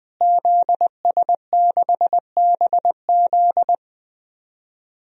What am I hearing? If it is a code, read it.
Morse code, heard as a keyed tone, about 20 wpm, 700 Hz. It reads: ZS6BZ
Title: DXCC-100 Call Signs - 5 Characters - Repeated 2x - 20wpm